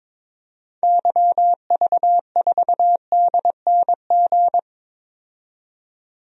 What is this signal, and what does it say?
Morse code, heard as a keyed tone, about 22 wpm, 700 Hz. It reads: YV4DNG
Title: DXCC-100 Call Signs - 6 Characters - 22wpm